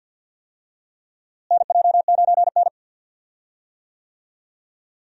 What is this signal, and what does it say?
Morse code, heard as a keyed tone, about 46 wpm, 700 Hz. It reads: NJ9N
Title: USA Call Signs - 4 Characters - Repeated 2x - 50wpm